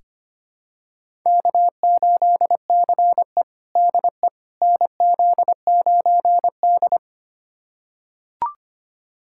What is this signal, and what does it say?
Morse code, heard as a keyed tone, about 25 wpm, 700 Hz. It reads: K8CE DE NZ9D
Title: USA Callsign DE Callsign - 4 Characters - 25wpm